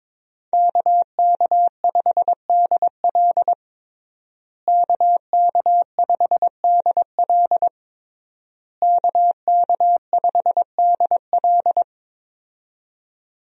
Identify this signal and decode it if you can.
Morse code, heard as a keyed tone, about 22 wpm, 700 Hz. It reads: KK5DL KK5DL KK5DL
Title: USA Call Signs - 5 Characters - Repeated 3x - 22wpm